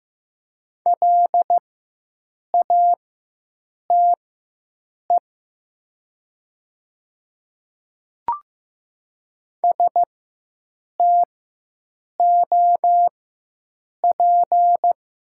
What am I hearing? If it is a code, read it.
Morse code, heard as a keyed tone, about 15 wpm, spaced as for 7 wpm, 700 Hz. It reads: LATE STOP
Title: 4x Character Spacing - Top 300 Words - 15wpm